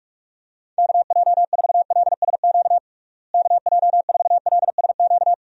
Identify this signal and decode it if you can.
Morse code, heard as a keyed tone, about 46 wpm, 700 Hz. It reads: KJ4LSQ KJ4LSQ
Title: USA Call Signs - 6 Characters - Repeated 2x - 45wpm